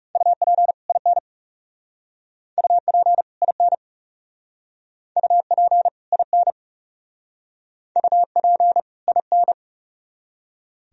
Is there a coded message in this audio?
Yes: Morse code from 0.2 to 9.5 s, heard as a keyed tone, about 38 wpm, 700 Hz. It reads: UP IN UP IN UP IN UP IN